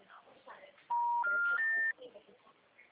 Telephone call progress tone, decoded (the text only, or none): special information tone